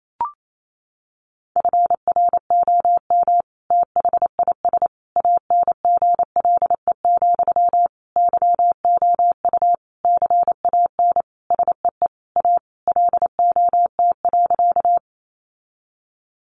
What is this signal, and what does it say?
Morse code, heard as a keyed tone, about 28 wpm, 700 Hz. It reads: FROM THIS ANGLE, YOU CAN SEE A LOT.
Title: Sentences from Top 1500 Words 28wpm